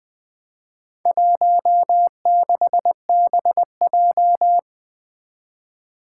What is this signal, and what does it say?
Morse code, heard as a keyed tone, about 20 wpm, 700 Hz. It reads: 16BJ
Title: Four Challenging Letter-Number Combinations 20wpm